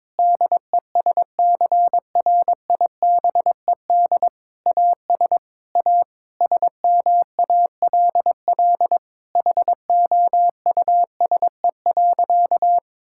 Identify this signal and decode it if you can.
Morse code, heard as a keyed tone, about 22 wpm, 700 Hz. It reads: DESCRIBED AS A SMALL HOUSE.
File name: practice / 22wpm / Sentences with 8-Word Phrases - 22wpm